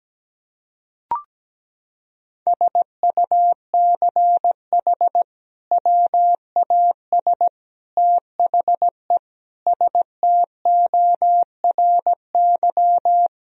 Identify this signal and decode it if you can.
Morse code, heard as a keyed tone, about 17 wpm, 700 Hz. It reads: SUCH WAS THE STORY
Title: Sentences from Top 1000 Words - Review Part 3 - 17wpm